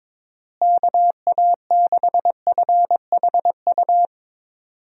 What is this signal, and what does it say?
Morse code, heard as a keyed tone, about 22 wpm, 700 Hz. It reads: KA6FHU
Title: USA Call Signs - 6 Characters - Repeated 3x - 22wpm